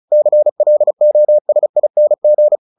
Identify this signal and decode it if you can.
Morse code, heard as a keyed tone, about 34 wpm, 600 Hz. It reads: CLOSING
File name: the old man and the sea 2 35wpm